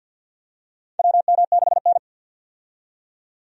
Morse code, heard as a keyed tone, about 46 wpm, 700 Hz. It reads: WM6N
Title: USA Call Signs - 4 Characters - 50wpm